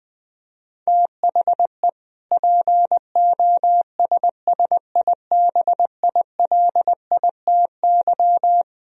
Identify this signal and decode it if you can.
Morse code, heard as a keyed tone, about 20 wpm, 700 Hz. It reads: THE POSSIBILITY
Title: Speed-Racing - Sets of 3 Words 17wpm